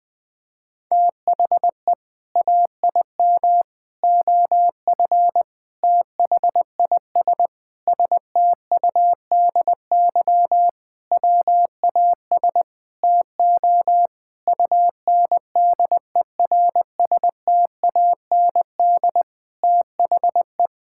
Morse code, heard as a keyed tone, about 20 wpm, 700 Hz. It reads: THE AIM OF THIS STUDY WAS TO UNDERSTAND THE
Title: Sentences with 7-Word Phrases - 20wpm